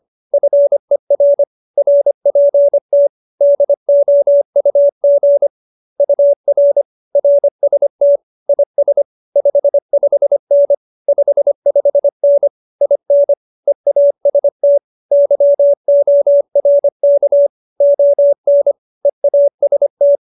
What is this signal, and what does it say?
Morse code, heard as a keyed tone, about 25 wpm, 570 Hz. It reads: FER RPT DOUG UR RST IS 55N 55N IN EAST YORK ON EAST